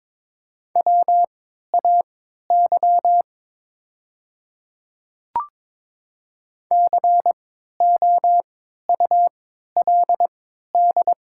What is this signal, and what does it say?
Morse code, heard as a keyed tone, about 22 wpm, spaced as for 12 wpm, 700 Hz. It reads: WAY COULD
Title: 3x Character Spacing - Top 100 Words - 22wpm